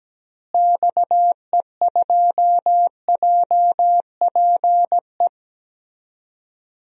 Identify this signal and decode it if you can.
Morse code, heard as a keyed tone, about 17 wpm, 700 Hz. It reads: XE2JPE
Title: DXCC-100 Call Signs - 6 Characters - Repeated 2x - 17wpm